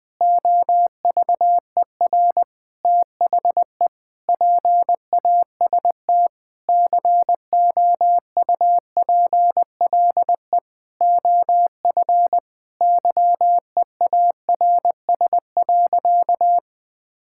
Morse code, heard as a keyed tone, about 20 wpm, 700 Hz. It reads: OVER THE PAST COUPLE OF YEARS.
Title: Sentences with 6-Word Phrases - Spoken Beginning - 20wpm